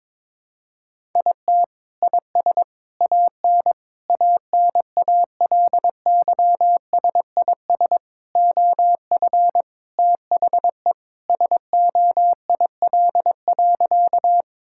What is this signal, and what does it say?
Morse code, heard as a keyed tone, about 22 wpm, 700 Hz. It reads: IT IS AN ANALYSIS OF THE SOIL.
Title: Sentences from Top 1000 Words - Review Part 5 - 22wpm